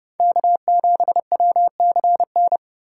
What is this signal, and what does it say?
Morse code, heard as a keyed tone, about 36 wpm, 700 Hz. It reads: K7WCN
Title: Speed Racing - USA 1x3 Call Signs 30wpm